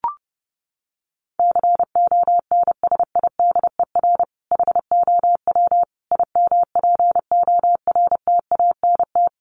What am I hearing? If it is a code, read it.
Morse code, heard as a keyed tone, about 30 wpm, 700 Hz. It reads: CONSIDER HOW IMPORTANT
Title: Sentences from Top 1000 Words 30wpm